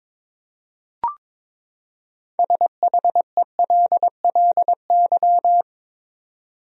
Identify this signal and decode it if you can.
Morse code, heard as a keyed tone, about 22 wpm, 700 Hz. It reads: SHELLY
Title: USA Common Names - 22wpm